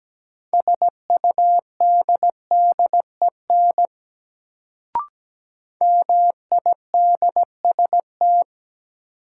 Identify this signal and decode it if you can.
Morse code, heard as a keyed tone, about 17 wpm, 700 Hz. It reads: SUDDEN MIDST
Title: Sets of 4 Words - New Word Drill - 17wpm